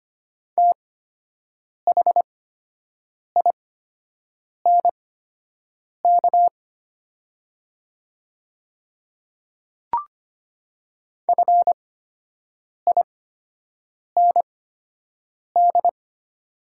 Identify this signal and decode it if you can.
Morse code, heard as a keyed tone, about 25 wpm, spaced as for 7 wpm, 700 Hz. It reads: THINK FIND
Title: 8x Character Spacing - Top 300 Words - 25wpm